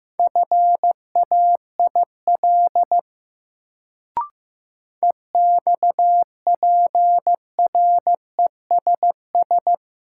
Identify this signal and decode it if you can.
Morse code, heard as a keyed tone, about 15 wpm, 700 Hz. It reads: FAIL EXPRESS
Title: Sentences from Top 1100 Words - New Word Drill Alt - 15wpm